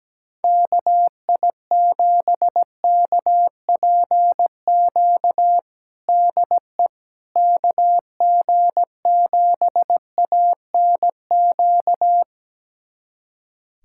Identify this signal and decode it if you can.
Morse code, heard as a keyed tone, about 17 wpm, 700 Hz. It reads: KI7KPQ DE KG7ANQ
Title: USA Callsign DE Callsign - 6 Characters - 17wpm